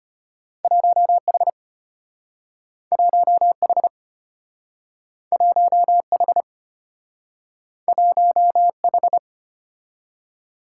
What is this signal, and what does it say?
Morse code, heard as a keyed tone, about 34 wpm, 700 Hz. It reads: 1H 1H 1H 1H